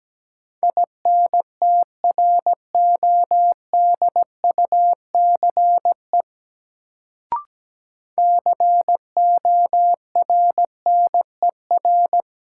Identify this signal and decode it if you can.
Morse code, heard as a keyed tone, about 17 wpm, 700 Hz. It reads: INTRODUCE CORNER
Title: Sentences from Top 1600 Words - New Word Drill Alt - 17wpm